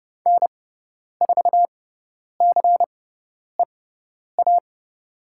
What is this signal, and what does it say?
Morse code, heard as a keyed tone, about 30 wpm, spaced as for 10 wpm, 700 Hz. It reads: N4CEA
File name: Lesson 17g - Letter u - Call Signs 30wpm